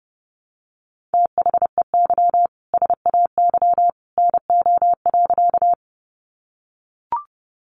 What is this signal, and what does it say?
Morse code, heard as a keyed tone, about 28 wpm, 700 Hz. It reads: THEY SAY NO.
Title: Easy Phrases 30wpm